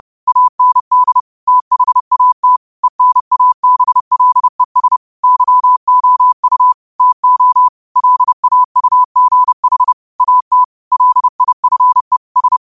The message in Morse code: AND THAT ENABLES YOU TO LAUGH AT LIFE S

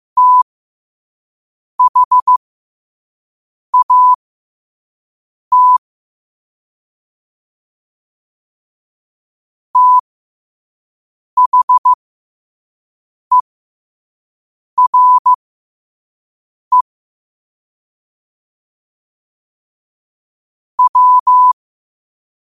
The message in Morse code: THAT THERE W

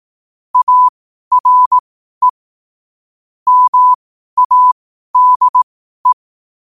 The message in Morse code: ARE MADE

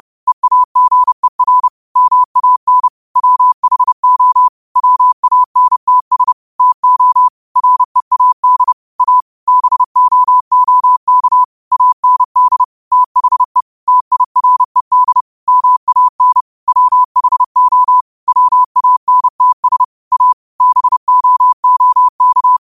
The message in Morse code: EAGER MAN WHO WANTS TO READ A BOOK AND THE TIRED MAN WHO WANTS A BOOK

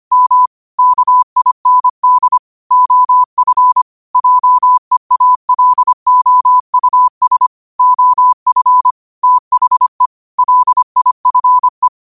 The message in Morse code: M KIND OF JEALOUS OF THE LIFE

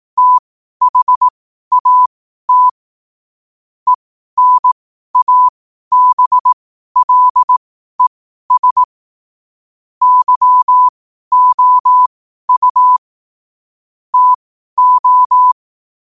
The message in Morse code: THAT ENABLES YOU TO

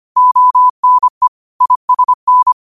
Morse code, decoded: ONE ISN

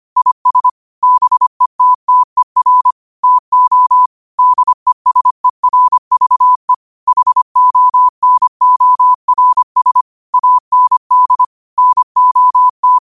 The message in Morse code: IS BETTER TO DESERVE HONORS AND NOT

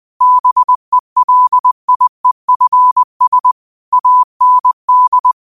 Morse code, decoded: BELIEFS AND

